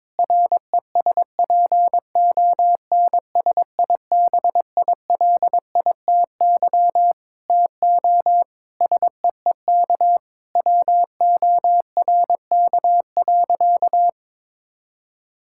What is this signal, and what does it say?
Morse code, heard as a keyed tone, about 22 wpm, 700 Hz. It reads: RESPONSIBILITY TO SEEK WORK.
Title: Sentences from Top 1600 Words 22wpm